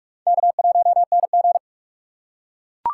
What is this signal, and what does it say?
Morse code, heard as a keyed tone, about 43 wpm, 700 Hz. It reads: K1NG
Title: USA Call Signs - 4 Characters - 2x Character Spacing - 45wpm